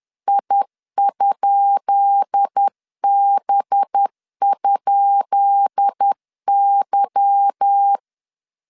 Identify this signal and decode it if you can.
Morse code, heard as a keyed tone, about 10 wpm, 800 Hz. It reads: I?B?Y
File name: LEZIONI / 10 SIGNS / 10B INT 10wpm 2844